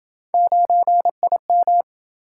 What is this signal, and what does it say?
Morse code, heard as a keyed tone, about 26 wpm, 700 Hz. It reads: 9IM